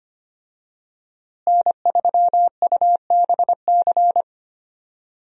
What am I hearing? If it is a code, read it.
Morse code, heard as a keyed tone, about 25 wpm, 700 Hz. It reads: N3UBC
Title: USA Call Signs - 5 Characters - Repeated 3x - 25wpm